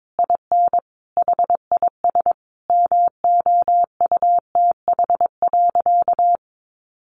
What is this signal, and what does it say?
Morse code, heard as a keyed tone, about 22 wpm, 700 Hz. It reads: IN HIS MOUTH.
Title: A Sound Of Thunder 22wpm